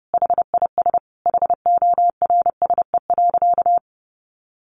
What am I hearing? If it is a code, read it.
Morse code, heard as a keyed tone, about 30 wpm, 700 Hz. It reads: HIS HORSE.
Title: Sentences from Top 1100 Words 30wpm